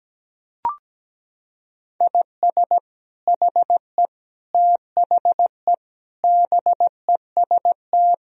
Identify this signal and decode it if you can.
Morse code, heard as a keyed tone, about 17 wpm, 700 Hz. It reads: IS HE THE BEST
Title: Sentences from Top 1000 Words - Review Part 4 - 17wpm